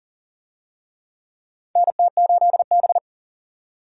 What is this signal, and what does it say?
Morse code, heard as a keyed tone, about 40 wpm, 700 Hz. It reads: NT8B
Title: USA Call Signs - 4 Characters - 40wpm